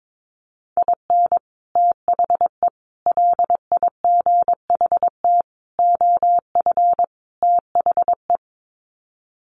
Morse code, heard as a keyed tone, about 22 wpm, 700 Hz. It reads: IN THE LIGHT OF THE
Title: Sets of 5 Words Encore2 22wpm